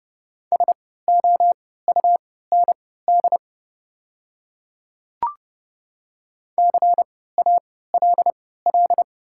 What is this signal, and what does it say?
Morse code, heard as a keyed tone, about 30 wpm, spaced as for 17 wpm, 690 Hz. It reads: SOUND CALL
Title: 3x Character Spacing - Top 300 Words - 30wpm